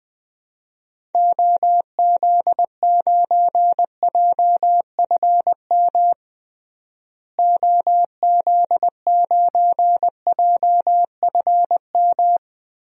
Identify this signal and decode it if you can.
Morse code, heard as a keyed tone, about 20 wpm, 700 Hz. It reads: OZ9JFM OZ9JFM
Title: DXCC-100 Call Signs - 6 Characters - Repeated 2x - 20wpm